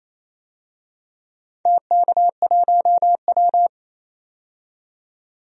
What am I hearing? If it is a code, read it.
Morse code, heard as a keyed tone, about 27 wpm, 690 Hz. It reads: TK1W
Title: DXCC-100 Call Signs - 4 Characters - 28wpm